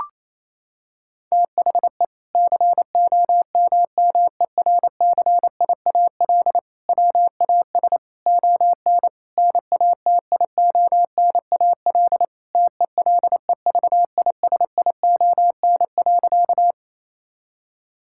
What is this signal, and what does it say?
Morse code, heard as a keyed tone, about 28 wpm, 700 Hz. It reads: THE COMMERCIAL WAS ON NATIONAL TELEVISION.
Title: Sentences from Top 1000 Words - Review Part 5 - 28wpm